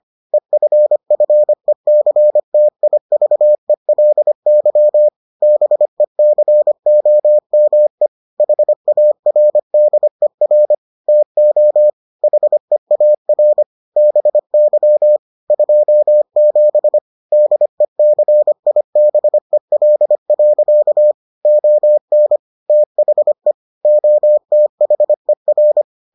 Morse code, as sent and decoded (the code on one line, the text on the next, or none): . ..-. ..-. -.-. - .. ...- . .-.. -.-- / -... -.-. --- -- . / .... .- .-. -.. .-. / - --- / .... .- .-. / -... -.-- / ..--- --... / -.. -.-. .. -... .-.. .-.-.- / --- -. / - .... / --- - .... .-.
EFFECTIVELY BECOME HARDER TO HEAR BY 27 DECIBEL. ON THE OTHER